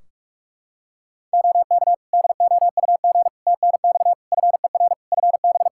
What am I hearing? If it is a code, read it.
Morse code, heard as a keyed tone, about 46 wpm, 700 Hz. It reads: OK DOUG TNX FER FB
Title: K1USN SST QSO Practice 45wpm